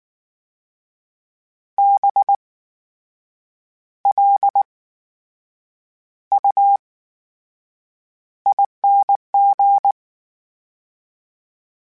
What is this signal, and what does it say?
Morse code, heard as a keyed tone, about 19 wpm, spaced as for 7 wpm, 800 Hz. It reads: BLUING
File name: Special INGs / Special INGs-mp3s / VerySlowToFaster Ing @18